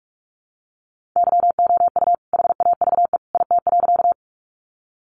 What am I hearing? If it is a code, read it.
Morse code, heard as a keyed tone, about 44 wpm, 700 Hz. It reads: YOU HAVE IT.
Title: Sentences from Top 100 Words 45wpm